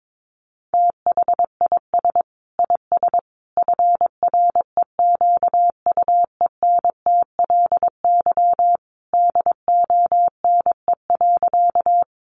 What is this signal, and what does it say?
Morse code, heard as a keyed tone, about 22 wpm, 700 Hz. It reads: THIS IS FREQUENTLY DONE.
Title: Sentences from Top 1300 Words 22wpm